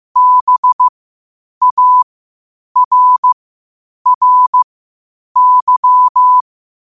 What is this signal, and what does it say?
Morse code, heard as a keyed tone, about 15 wpm, spaced as for 8 wpm, 1000 Hz. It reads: BARRY
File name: qotdfiles / 8wpm